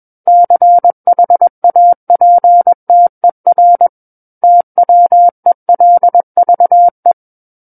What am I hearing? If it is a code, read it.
Morse code, heard as a keyed tone, about 21 wpm, 700 Hz. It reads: CHAPTER TWELVE
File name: Book One / 12-Chapter 12 - 21 WPM